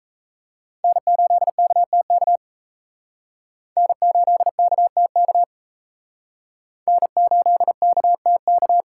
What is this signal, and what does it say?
Morse code, heard as a keyed tone, about 34 wpm, 700 Hz. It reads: N8KTK N8KTK N8KTK
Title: Speed Racing - USA Call Signs 28wpm